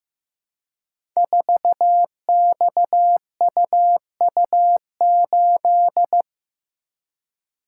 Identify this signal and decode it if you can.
Morse code, heard as a keyed tone, about 15 wpm, 700 Hz. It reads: 4XUU8
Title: Code Groups - 5 Letter:Numbers - 15wpm